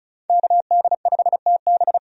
Morse code, heard as a keyed tone, about 34 wpm, 700 Hz. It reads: KD5TB
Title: USA Call Signs - 5 Characters - 35wpm